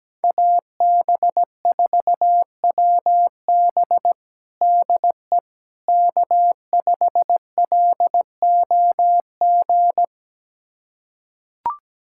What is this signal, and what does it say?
Morse code, heard as a keyed tone, about 17 wpm, 700 Hz. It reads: AB4WB DE K5LOG
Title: USA Callsign DE Callsign - 5 Characters - 17wpm